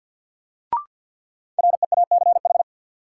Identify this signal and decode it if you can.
Morse code, heard as a keyed tone, about 53 wpm, 700 Hz. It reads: REACH
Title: Sentences from Top 400 Words - New Word Drill - 50wpm